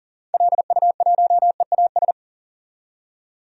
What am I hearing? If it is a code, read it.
Morse code, heard as a keyed tone, about 40 wpm, 700 Hz. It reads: LU1EAS
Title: DXCC-100 Call Signs - 6 Characters - 40wpm